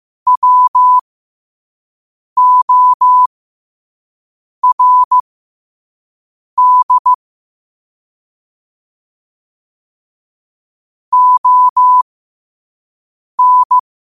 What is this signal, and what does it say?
Morse code, heard as a keyed tone, about 15 wpm, spaced as for 5 wpm, 1000 Hz. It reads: WORD ON
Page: New quotes every day in morse code at 5 Words per minute.